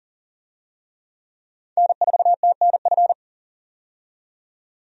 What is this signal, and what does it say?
Morse code, heard as a keyed tone, about 40 wpm, 700 Hz. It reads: N4TNF
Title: USA Call Signs - 5 Characters - 40wpm